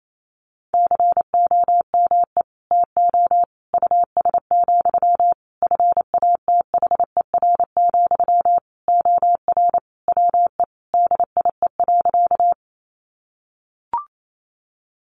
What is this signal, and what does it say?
Morse code, heard as a keyed tone, about 28 wpm, 700 Hz. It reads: COME TO US, FATHER, OR WE DIE.
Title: Sentences from Top 1300 Words 28wpm